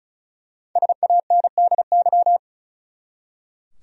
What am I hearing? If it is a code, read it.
Morse code, heard as a keyed tone, about 36 wpm, 700 Hz. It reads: SANDY
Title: USA Common Names - Max 5 Letters - 35wpm